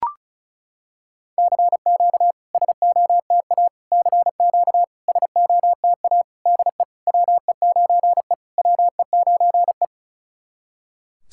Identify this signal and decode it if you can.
Morse code, heard as a keyed tone, about 36 wpm, 700 Hz. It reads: CQ SOTA CQ SOTA DE WE9E WE9E
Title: SOTA QSO Practice 35wpm